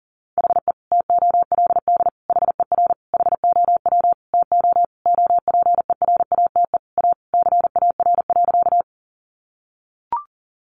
Morse code, heard as a keyed tone, about 40 wpm, 700 Hz. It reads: HE TOLD HER HOW TO OPERATE A CAR.
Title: Sentences from Top 1500 Words 40wpm